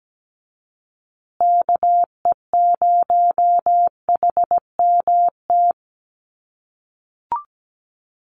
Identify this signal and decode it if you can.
Morse code, heard as a keyed tone, about 17 wpm, 700 Hz. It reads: KE0HMT
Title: USA Call Signs Repeated 2x 17wpm